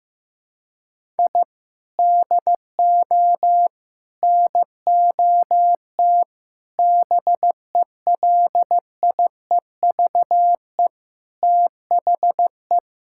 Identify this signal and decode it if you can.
Morse code, heard as a keyed tone, about 14 wpm, 700 Hz. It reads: I DO NOT BELIEVE THE
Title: Sentences from Top 1000 Words - Review Part 6 - 15wpm